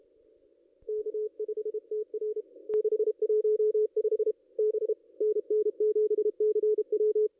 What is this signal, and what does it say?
Morse code, heard as a keyed tone, about 32 wpm, 440 Hz. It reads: K5TR 615 B NN7CW
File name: contests / ss-cw-k5tr-op-n5rz -0500